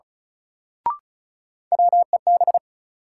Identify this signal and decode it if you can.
Morse code, heard as a keyed tone, about 34 wpm, 700 Hz. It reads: WEB
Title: Sentences from Top 1000 Words - Review Part 4 - New Word Drill - 35wpm